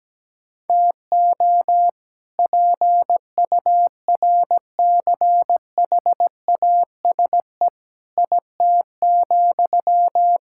Morse code, heard as a keyed tone, about 17 wpm, 700 Hz. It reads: TO PURCHASE IT,
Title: Sentences from Top 1000 Words - Review Part 5 - 17wpm